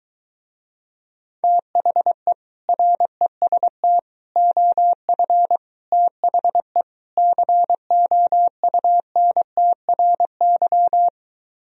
Morse code, heard as a keyed tone, about 23 wpm, 700 Hz. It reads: THE REST OF THE COUNTRY